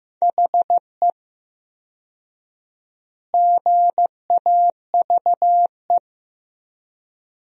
Morse code, heard as a keyed tone, about 15 wpm, 700 Hz. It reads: HE GAVE
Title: 4x Word-Spacing - Sentences from Top 500 Words - 15wpm